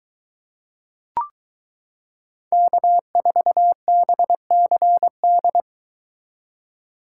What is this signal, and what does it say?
Morse code, heard as a keyed tone, about 22 wpm, 700 Hz. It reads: K4BCD